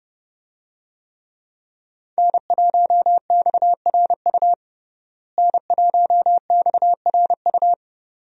Morse code, heard as a keyed tone, about 30 wpm, 700 Hz. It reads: N1XRU N1XRU
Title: USA Call Signs - 5 Characters - Repeated 2x - 30wpm